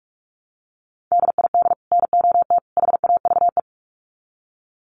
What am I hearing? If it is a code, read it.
Morse code, heard as a keyed tone, about 44 wpm, 700 Hz. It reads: DID NOT HAVE
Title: Sets of 3 Words 45wpm